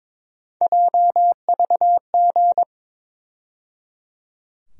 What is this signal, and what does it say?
Morse code, heard as a keyed tone, about 22 wpm, 700 Hz. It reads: JVG
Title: USA Call Sign Postfixes - 3 Letters 22wpm